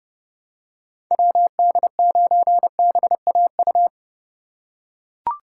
Morse code, heard as a keyed tone, about 32 wpm, 700 Hz. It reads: WD9BAU
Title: USA Call Signs - 6 Characters - 3x Character Spacing - 30wpm